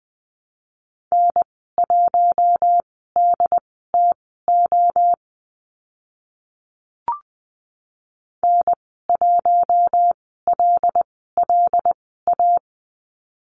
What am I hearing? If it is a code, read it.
Morse code, heard as a keyed tone, about 20 wpm, spaced as for 14 wpm, 700 Hz. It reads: N1DTO N1LLA
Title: 076 Lesson 13g - Letter L - Call Signs 20wpm